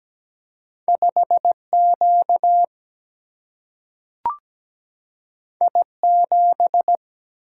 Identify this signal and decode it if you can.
Morse code, heard as a keyed tone, about 17 wpm, 700 Hz. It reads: 5Q I7
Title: Single-Letter Single-Number 17wpm